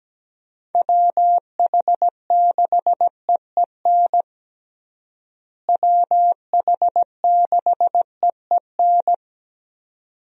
Morse code, heard as a keyed tone, about 17 wpm, 700 Hz. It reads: WH6EEN WH6EEN
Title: USA Call Signs - 6 Characters - Repeated 2x - 17wpm